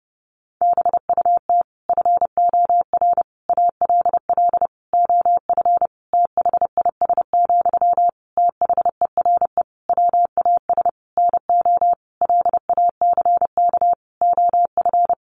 Morse code, heard as a keyed tone, about 30 wpm, 700 Hz. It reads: BUT FOR ALL OF THIS, THERE WAS NO LACK OF
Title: Sentences from Top 1300 Words 30wpm